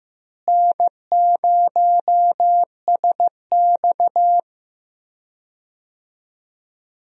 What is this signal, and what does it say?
Morse code, heard as a keyed tone, about 15 wpm, 700 Hz. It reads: N0SX
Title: USA Call Signs - 4 Characters - Repeated 3x - 15wpm